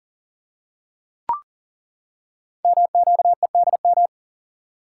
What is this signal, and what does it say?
Morse code, heard as a keyed tone, about 39 wpm, 700 Hz. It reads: MQEDM
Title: Code Groups - 5 Letter:Numbers - 40wpm